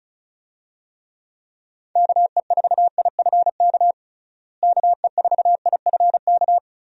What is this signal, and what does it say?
Morse code, heard as a keyed tone, about 34 wpm, 700 Hz. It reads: KE4IFK KE4IFK
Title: USA Call Signs - 6 Characters - Repeated 3x - 35wpm